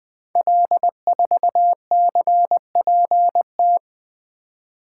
Morse code, heard as a keyed tone, about 20 wpm, 700 Hz. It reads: L4CPT
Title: Code Groups - 5 Letter:Numbers - 20wpm